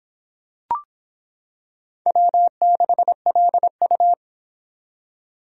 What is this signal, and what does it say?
Morse code, heard as a keyed tone, about 26 wpm, 700 Hz. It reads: W6LU